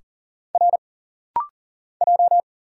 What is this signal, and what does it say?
Morse code, heard as a keyed tone, about 40 wpm, 700 Hz. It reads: R J
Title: Single Letters - Rapid-Fire 40wpm